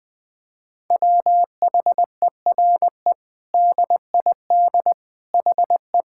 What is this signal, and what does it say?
Morse code, heard as a keyed tone, about 20 wpm, 700 Hz. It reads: WHERE DID HE
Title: Sentences from Top 1000 Words - Review Part 5 - 20wpm